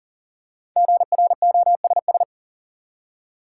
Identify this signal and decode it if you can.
Morse code, heard as a keyed tone, about 40 wpm, 700 Hz. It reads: GROSS
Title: Five Letter Words - 40wpm